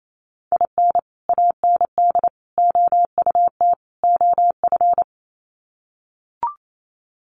Morse code, heard as a keyed tone, about 28 wpm, 700 Hz. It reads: IN AND OUT OF
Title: Sets of 4 Words 28wpm